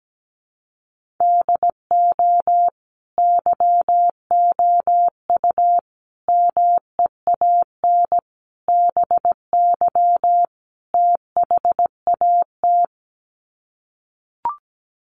Morse code, heard as a keyed tone, about 17 wpm, 700 Hz. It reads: DO YOU MEAN BY THAT
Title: Sets of 5 Words Encore2 17wpm